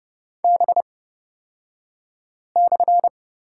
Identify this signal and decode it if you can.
Morse code, heard as a keyed tone, about 30 wpm, spaced as for 5 wpm, 700 Hz. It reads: B/
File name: Lesson 38d - Letter q - Two Characters 30wpm